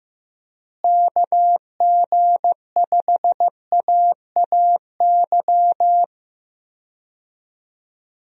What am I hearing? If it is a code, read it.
Morse code, heard as a keyed tone, about 16 wpm, 700 Hz. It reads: KG5AAY
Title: USA Call Signs - 6 Characters - Repeated 3x - 15wpm